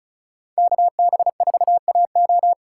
Morse code, heard as a keyed tone, about 39 wpm, 700 Hz. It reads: KB4AO
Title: Speed Racing - USA Call Signs 35wpm